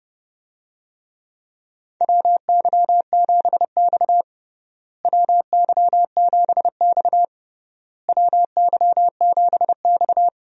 Morse code, heard as a keyed tone, about 30 wpm, 700 Hz. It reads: WY7X WY7X WY7X
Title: USA Call Signs - 4 Characters - Repeated 3x - 30wpm